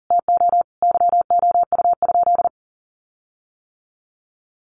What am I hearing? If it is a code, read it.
Morse code, heard as a keyed tone, about 37 wpm, 700 Hz. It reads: TO YOU?
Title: Sentences from Top 1300 Words 40wpm